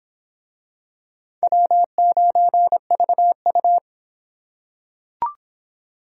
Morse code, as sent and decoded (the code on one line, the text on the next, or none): .-- ----. ...- ..-
W9VU